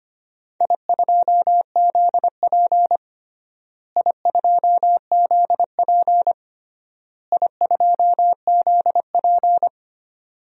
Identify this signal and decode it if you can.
Morse code, heard as a keyed tone, about 25 wpm, 700 Hz. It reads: I2ZP I2ZP I2ZP
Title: DXCC-100 Call Signs - 4 Characters - Repeated 3x - 25wpm